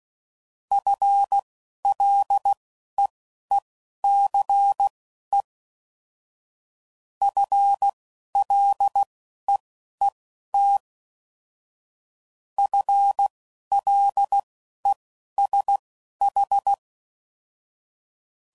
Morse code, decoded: FLEECE FLEET FLESH

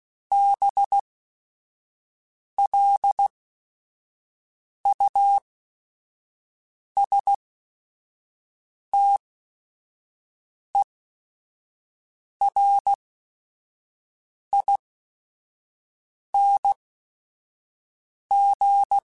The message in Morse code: BLUSTERING

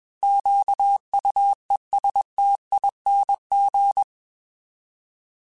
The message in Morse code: QUESTING

